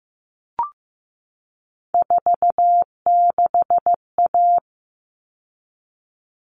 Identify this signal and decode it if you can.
Morse code, heard as a keyed tone, about 15 wpm, 700 Hz. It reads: 46A
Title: All Cut Numbers - 3-Digit 15wpm